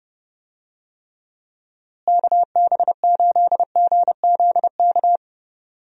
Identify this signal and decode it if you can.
Morse code, heard as a keyed tone, about 30 wpm, 700 Hz. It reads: KB8GZK
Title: USA Call Signs - 6 Characters - Repeated 2x - 30wpm